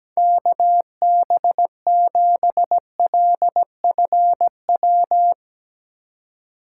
Morse code, heard as a keyed tone, about 17 wpm, 700 Hz. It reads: KB7LFW
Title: Speed Racing - USA 2x3 Call Signs 17wpm